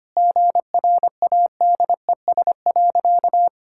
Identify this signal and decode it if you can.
Morse code, heard as a keyed tone, about 25 wpm, 700 Hz. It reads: GRADES.
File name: practice / 25wpm / Sentences with 4-Word Phrases - 25wpm